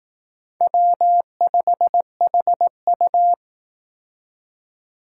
Morse code, heard as a keyed tone, about 18 wpm, 700 Hz. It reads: W5HU
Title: Speed Racing - USA 1x2 Call Signs 15wpm